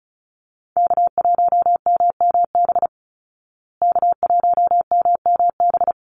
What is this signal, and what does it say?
Morse code, heard as a keyed tone, about 34 wpm, 700 Hz. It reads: K1MMB K1MMB
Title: Call Signs Repeated 2x - 35wpm